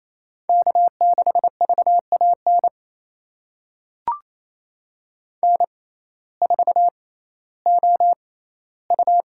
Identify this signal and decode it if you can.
Morse code, heard as a keyed tone, about 28 wpm, spaced as for 15 wpm, 700 Hz. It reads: K6VAN N4OU
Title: USA Call Signs - 5 Characters - 6x Character Spacing - 28wpm